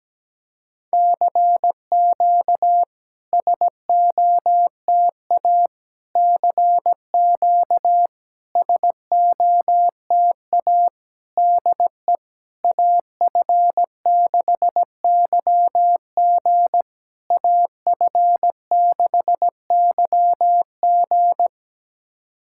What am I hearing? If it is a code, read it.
Morse code, heard as a keyed tone, about 17 wpm, 700 Hz. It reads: CQ SOTA CQ SOTA DE AF6YG AF6YG
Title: SOTA QSO Practice 17wpm